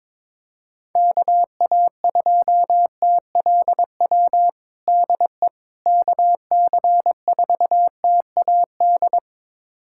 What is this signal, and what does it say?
Morse code, heard as a keyed tone, about 22 wpm, 700 Hz. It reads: KA2TLW DE KC4TAD
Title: USA Callsign DE Callsign - 6 Characters - 22wpm